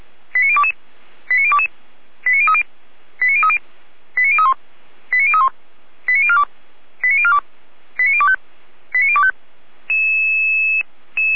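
Five-tone selective call (ZVEI), heard as quick runs of five tones, tones of about 70 ms. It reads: 89022 89022 89030 89030 89031 89031 89042 89042 89025 89025
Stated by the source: Five-tone ZVEI selective call received by SDR.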